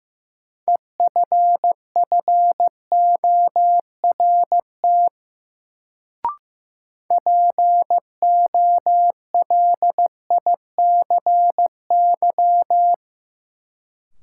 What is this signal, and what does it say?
Morse code, heard as a keyed tone, about 15 wpm, 700 Hz. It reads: EFFORT POLICY
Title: Sentences from Top 1200 Words - New Word Drill Alt - 15wpm